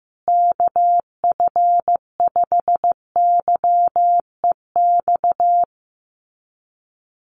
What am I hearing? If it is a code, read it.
Morse code, heard as a keyed tone, about 15 wpm, 700 Hz. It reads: KF5YEX
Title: USA Call Signs Repeated 3x 15wpm